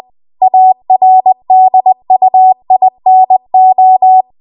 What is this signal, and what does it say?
Morse code, heard as a keyed tone, about 20 wpm, 750 Hz. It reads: ARDUINO